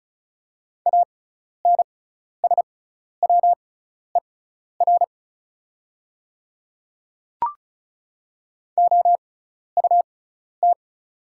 Morse code, heard as a keyed tone, about 36 wpm, spaced as for 12 wpm, 690 Hz. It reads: ANSWER OUT
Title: 6x Character Spacing - Top 300 Words - 35wpm